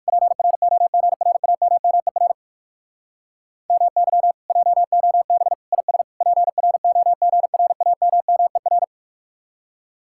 Morse code, heard as a keyed tone, about 57 wpm, 700 Hz. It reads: PROGRAMMER MY JOB IS PROGRAMMER